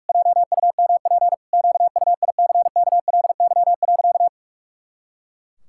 Morse code, heard as a keyed tone, about 48 wpm, 700 Hz. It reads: JUMP QUICKLY.
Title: Pangram Challenge v2 - 45wpm